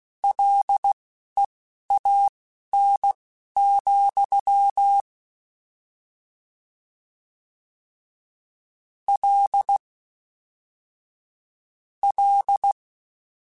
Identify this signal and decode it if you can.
Morse code, heard as a keyed tone, about 16 wpm, spaced as for 12 wpm, 800 Hz. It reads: LEAN, L L